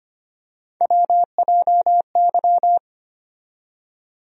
Morse code, heard as a keyed tone, about 25 wpm, 700 Hz. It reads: WJY
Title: USA Call Sign Postfixes - 3 Letters 25wpm